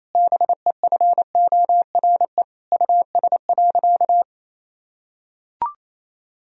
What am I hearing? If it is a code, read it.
Morse code, heard as a keyed tone, about 28 wpm, 700 Hz. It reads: BEFORE US.
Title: Sentences from Top 1000 Words - Review Part 1 - 28wpm